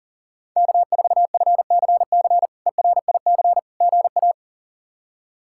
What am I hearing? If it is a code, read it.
Morse code, heard as a keyed tone, about 40 wpm, 700 Hz. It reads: K4FCC ERIC GA
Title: K1USN SST QSO Practice - 40wpm